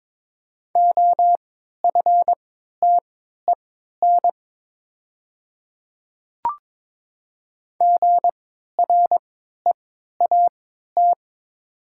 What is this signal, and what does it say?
Morse code, heard as a keyed tone, about 22 wpm, spaced as for 12 wpm, 700 Hz. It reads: OFTEN GREAT
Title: 3x Character Spacing - Top 300 Words - 22wpm